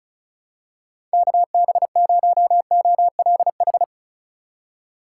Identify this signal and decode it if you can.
Morse code, heard as a keyed tone, about 34 wpm, 700 Hz. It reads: KB0OLH
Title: USA Call Signs - 6 Characters - 6x Character Spacing - 35wpm